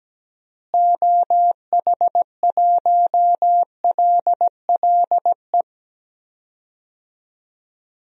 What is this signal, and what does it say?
Morse code, heard as a keyed tone, about 17 wpm, 700 Hz. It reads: OH1LLE
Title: DXCC-100 Call Signs - 6 Characters - Repeated 2x - 17wpm